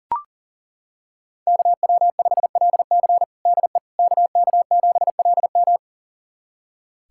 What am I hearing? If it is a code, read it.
Morse code, heard as a keyed tone, about 40 wpm, 700 Hz. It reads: KW5LC DE KK7LM
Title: USA Callsign DE Callsign - 5 Characters - 40wpm